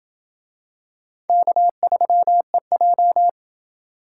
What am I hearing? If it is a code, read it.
Morse code, heard as a keyed tone, about 25 wpm, 700 Hz. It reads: K3EJ
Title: Speed Racing - USA 1x2 Call Signs 20wpm